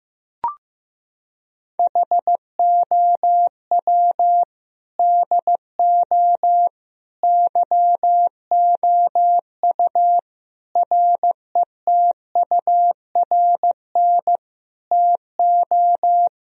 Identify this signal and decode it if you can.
Morse code, heard as a keyed tone, about 15 wpm, 700 Hz. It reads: HOW DO YOU RETURN TO
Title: Sentences from Top 1000 Words - Review Part 3 - 15wpm